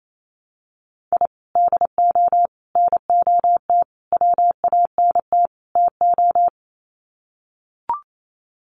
Morse code, heard as a keyed tone, about 28 wpm, 700 Hz. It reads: I DO NOT WANT TO
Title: Sets of 5 Words Encore2 28wpm